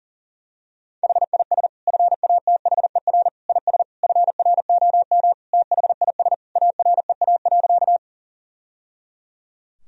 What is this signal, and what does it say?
Morse code, heard as a keyed tone, about 40 wpm, 700 Hz. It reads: HIS FATHER IS FROM THIS AREA.
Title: Sentences from Top 1000 Words - Review Part 2 - 40wpm